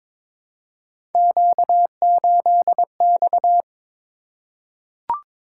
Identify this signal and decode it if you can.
Morse code, heard as a keyed tone, about 22 wpm, 700 Hz. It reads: Q8X
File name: practice / 22wpm / Three Letter-Number Combinations - 22wpm